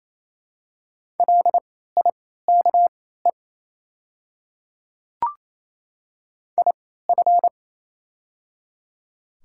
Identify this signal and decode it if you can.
Morse code, heard as a keyed tone, about 28 wpm, spaced as for 16 wpm, 690 Hz. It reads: LIKE IF
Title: 3x Character Spacing - Top 100 Words - 28wpm